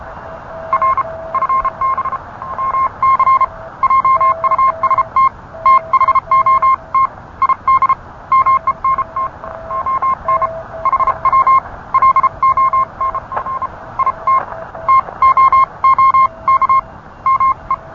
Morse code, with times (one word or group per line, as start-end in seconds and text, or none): 0.7-1.0 R
1.3-2.2 FB
2.6-3.4 MC
3.8-5.3 JUST
5.7-7.1 THOT
7.4-7.9 ID
8.3-9.3 MENT
9.7-10.5 ON
10.9-11.6 HW
11.9-13.7 LONF
14.0-14.4 IT
14.9-16.8 TOOK
17.3-17.7 ME